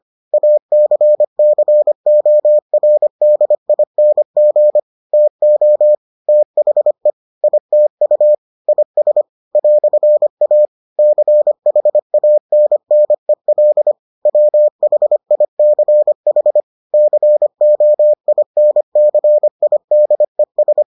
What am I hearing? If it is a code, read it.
Morse code, heard as a keyed tone, about 25 wpm, 600 Hz. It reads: ACCORDING TO THE ITU IS "A CHANNEL WHICH COINCIDES